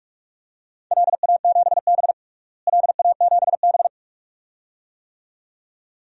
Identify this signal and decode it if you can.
Morse code, heard as a keyed tone, about 41 wpm, 700 Hz. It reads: LA7B LA7B
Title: DXCC-100 Call Signs - 4 Characters - Repeated 2x - 45wpm